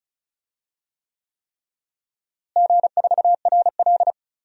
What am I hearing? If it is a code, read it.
Morse code, heard as a keyed tone, about 34 wpm, 700 Hz. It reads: G4RL
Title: DXCC-100 Call Signs - 4 Characters - Repeated 3x - 35wpm